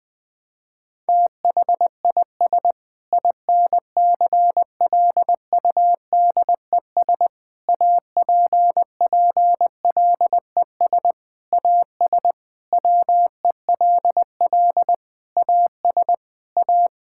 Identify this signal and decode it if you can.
Morse code, heard as a keyed tone, about 20 wpm, 700 Hz. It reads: THIS INCLUDES APPLES AS WELL AS A
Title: Sentences with 7-Word Phrases - 20wpm